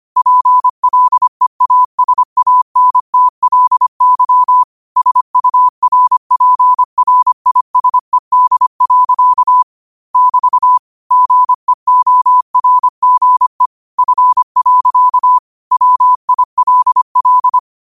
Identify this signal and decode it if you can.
Morse code, heard as a keyed tone, about 25 wpm, 1000 Hz. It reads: PLEASANTLY SURPRISED. <BT> GEORGE F. WILL
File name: qotdfiles / 25wpm